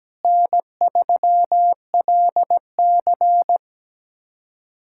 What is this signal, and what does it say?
Morse code, heard as a keyed tone, about 17 wpm, 700 Hz. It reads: N3LC
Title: USA Callsign DE Callsign - 4 Characters - 17wpm